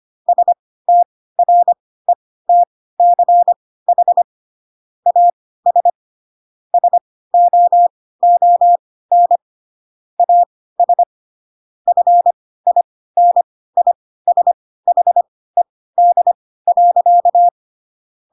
Morse code, heard as a keyed tone, about 25 wpm, spaced as for 16 wpm, 700 Hz. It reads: STRETCH AS SOON AS FINISHED.
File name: Pinocchio / Pinocchio ch3 10 25 ik1hge